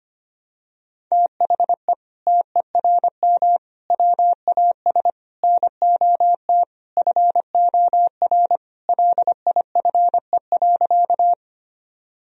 Morse code, heard as a keyed tone, about 25 wpm, 700 Hz. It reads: THE TERM WAS NOT FOR LIFE.
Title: Sentences from Top 1000 Words - Review Part 4 - 25wpm